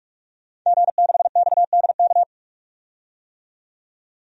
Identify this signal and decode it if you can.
Morse code, heard as a keyed tone, about 43 wpm, 700 Hz. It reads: G6XDK
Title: DXCC-100 Call Signs - 5 Characters - 45wpm